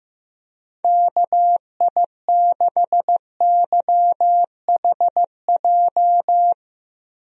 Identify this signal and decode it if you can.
Morse code, heard as a keyed tone, about 15 wpm, 700 Hz. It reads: KI6YHJ
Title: USA Call Signs - 6 Characters - Repeated 2x - 15wpm